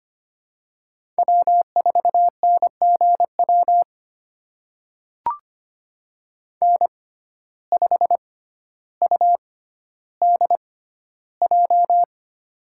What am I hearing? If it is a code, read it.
Morse code, heard as a keyed tone, about 25 wpm, spaced as for 13 wpm, 700 Hz. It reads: W4NGW N5UDJ
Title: USA Call Signs - 5 Characters - 6x Character Spacing - 25wpm